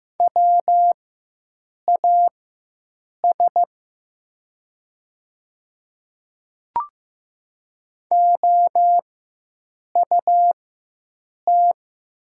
Morse code, heard as a keyed tone, about 15 wpm, spaced as for 7 wpm, 700 Hz. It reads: WAS OUT
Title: 4x Character Spacing - Top 100 Words - 15wpm